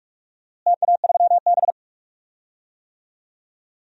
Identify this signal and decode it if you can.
Morse code, heard as a keyed tone, about 41 wpm, 700 Hz. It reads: TA3B
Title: DXCC-100 Call Signs - 4 Characters - Repeated 2x - 45wpm